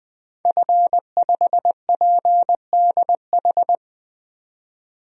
Morse code, heard as a keyed tone, about 21 wpm, 700 Hz. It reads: F5PDH
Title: DXCC-100 Call Signs - 5 Characters - 20wpm